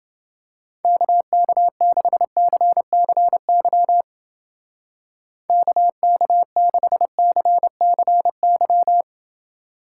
Morse code, heard as a keyed tone, about 28 wpm, 700 Hz. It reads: KK6CCY KK6CCY